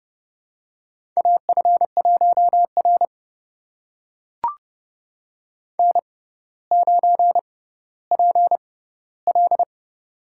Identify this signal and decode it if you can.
Morse code, heard as a keyed tone, about 30 wpm, spaced as for 15 wpm, 700 Hz. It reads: AF1R N9PL
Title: USA Call Signs - 4 Characters - 6x Character Spacing - 30wpm